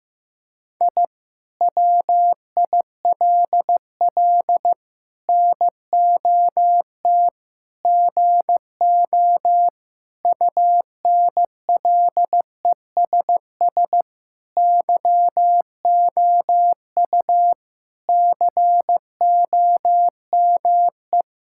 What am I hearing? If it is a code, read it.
Morse code, heard as a keyed tone, about 15 wpm, 700 Hz. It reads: I WILL NOT GO UNLESS YOU COME
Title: Sentences from Top 1000 Words - Review Part 6 - 15wpm